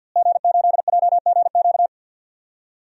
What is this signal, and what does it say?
Morse code, heard as a keyed tone, about 47 wpm, 700 Hz. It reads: G8JGQ
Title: DXCC-100 Call Signs - 5 Characters - Repeated 2x - 50wpm